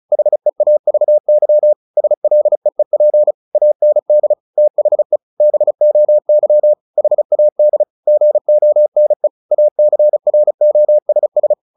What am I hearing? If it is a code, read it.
Morse code, heard as a keyed tone, about 34 wpm, 600 Hz. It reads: HEAVY SLEEP AND THE BOY HAD GONE ACROSS
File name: the old man and the sea 2 35wpm